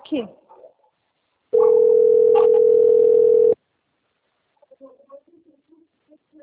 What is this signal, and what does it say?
Ringback tone (North America), 440 + 480 Hz, with speech.
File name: rg-1600-unknown- - - .54106